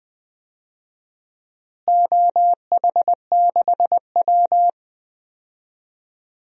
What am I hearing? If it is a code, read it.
Morse code, heard as a keyed tone, about 20 wpm, 700 Hz. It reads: OH6W
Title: DXCC-100 Call Signs - 4 Characters - Repeated 3x - 20wpm